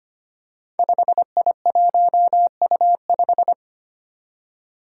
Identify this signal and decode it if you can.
Morse code, heard as a keyed tone, about 25 wpm, 700 Hz. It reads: 5I1U5
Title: Five Challenging Letter-Number Combinations 25wpm